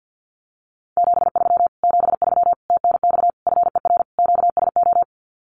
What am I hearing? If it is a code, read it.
Morse code, heard as a keyed tone, about 50 wpm, 700 Hz. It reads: 73 73 TNX FER QSO
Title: Common QSO Phrases Encore 50wpm